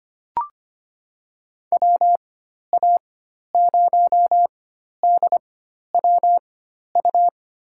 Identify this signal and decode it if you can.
Morse code, heard as a keyed tone, about 25 wpm, spaced as for 12 wpm, 700 Hz. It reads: WA0DWU
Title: USA Call Signs - 6 Characters - 4x Character Spacing - 25wpm